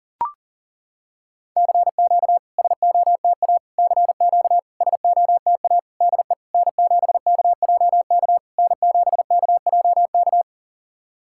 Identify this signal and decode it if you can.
Morse code, heard as a keyed tone, about 40 wpm, 700 Hz. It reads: CQ SOTA CQ SOTA DE N7KJK N7KJK
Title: SOTA QSO Practice 40wpm